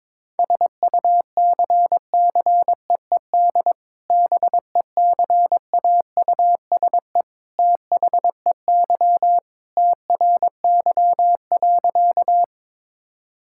Morse code, heard as a keyed tone, about 22 wpm, 700 Hz. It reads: SUCCEED BECAUSE THEY TRY.
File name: practice / 22wpm / Sentences with 2-Word Phrases - 22wpm